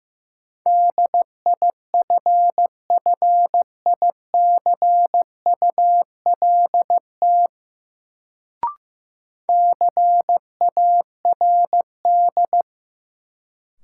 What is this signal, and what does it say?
Morse code, heard as a keyed tone, about 16 wpm, 700 Hz. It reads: DIFFICULT CARD
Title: Sentences from Top 600 Words - New Word Drill - 15wpm